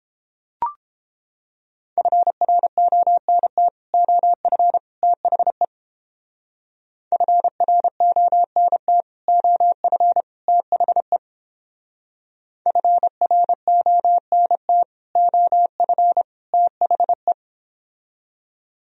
Morse code, heard as a keyed tone, about 31 wpm, 700 Hz. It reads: FRONT OF THE FRONT OF THE FRONT OF THE